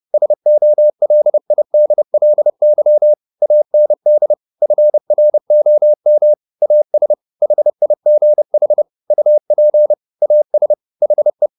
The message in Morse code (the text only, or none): SOLIDLY AND FROM AS HIGH UP AS HE